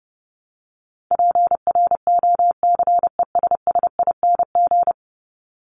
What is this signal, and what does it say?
Morse code, heard as a keyed tone, about 30 wpm, 700 Hz. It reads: PROCESSING
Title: ING Suffix - Top 100 Words with Base Words 30wpm